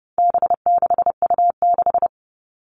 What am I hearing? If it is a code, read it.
Morse code, heard as a keyed tone, about 30 wpm, 700 Hz. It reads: B6U6
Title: All Cut Numbers - 4-Digit 30wpm